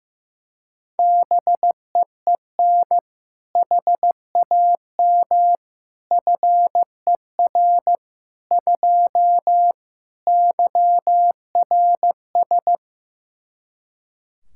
Morse code, heard as a keyed tone, about 15 wpm, 700 Hz. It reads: BEEN HAM FER 2 YRS
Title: K1USN SST QSO Practice 15wpm